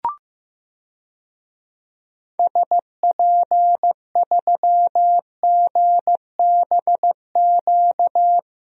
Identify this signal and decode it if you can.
Morse code, heard as a keyed tone, about 16 wpm, 700 Hz. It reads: SP3GBQ
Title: DXCC-100 Call Signs - 6 Characters - 15wpm